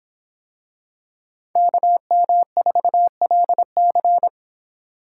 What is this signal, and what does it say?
Morse code, heard as a keyed tone, about 26 wpm, 700 Hz. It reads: KM4LC